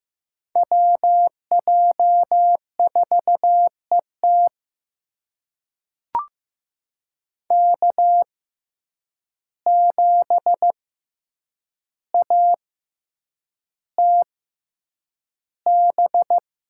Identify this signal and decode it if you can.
Morse code, heard as a keyed tone, about 15 wpm, spaced as for 8 wpm, 700 Hz. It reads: WJ4ET K7ATB
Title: USA Call Signs - 5 Characters - 6x Character Spacing - 15wpm